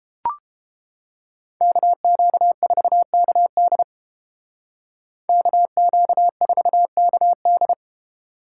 Morse code, heard as a keyed tone, about 29 wpm, 700 Hz. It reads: KQ4KD KQ4KD